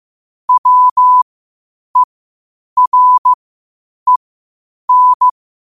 Morse code, heard as a keyed tone, about 15 wpm, spaced as for 8 wpm, 1000 Hz. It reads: WEREN
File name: qotdfiles / 8wpm